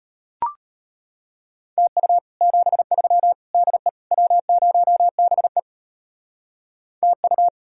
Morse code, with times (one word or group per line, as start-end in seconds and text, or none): 1.8-2.2 TU
2.4-3.3 73
3.5-3.9 DE
4.1-5.6 W0BE
7.0-7.5 TU